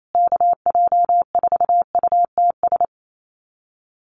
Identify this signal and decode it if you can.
Morse code, heard as a keyed tone, about 28 wpm, 700 Hz. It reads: KJ4UTS
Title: USA Call Signs Repeated 3x 28wpm